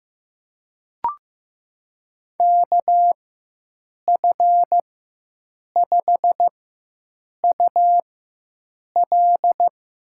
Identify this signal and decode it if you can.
Morse code, heard as a keyed tone, about 15 wpm, spaced as for 7 wpm, 700 Hz. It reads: KF5UL
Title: USA Call Signs - 4x Character Spacing - 15wpm